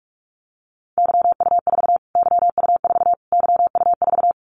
Morse code, heard as a keyed tone, about 44 wpm, 700 Hz. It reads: YU4 YU4 YU4
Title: Callsign Prefixes for Top 25 DXCC - Repeated 3x - 45wpm